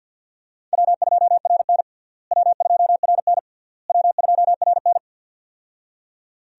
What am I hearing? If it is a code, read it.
Morse code, heard as a keyed tone, about 51 wpm, 700 Hz. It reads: W2RN W2RN W2RN
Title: USA Call Signs - 4 Characters - Repeated 3x - 50wpm